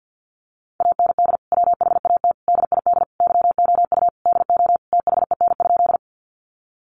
Morse code, heard as a keyed tone, about 50 wpm, 700 Hz. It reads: AND WHAT DID YOU DO THEN?
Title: Sentences from Top 100 Words 50wpm